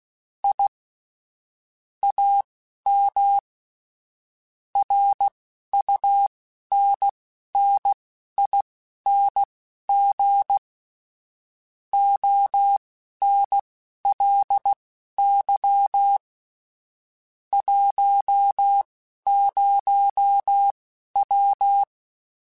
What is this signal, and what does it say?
Morse code, heard as a keyed tone, about 16 wpm, spaced as for 11 wpm, 800 Hz. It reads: I AM RUNNING ONLY 10W